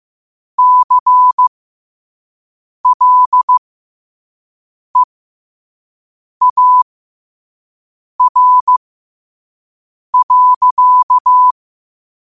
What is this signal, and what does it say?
Morse code, heard as a keyed tone, about 15 wpm, spaced as for 5 wpm, 1000 Hz. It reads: CLEAR.
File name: qotdfiles / 5wpm